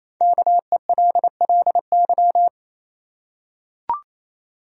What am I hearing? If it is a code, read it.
Morse code, heard as a keyed tone, about 28 wpm, 700 Hz. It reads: KELLY
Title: USA Common Names - 28wpm